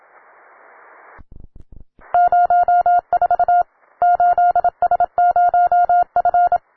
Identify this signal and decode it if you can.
Morse code, heard as a keyed tone, about 27 wpm, 710 Hz. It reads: 04 8S0F
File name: wae / cw beginner / ua9cdv cw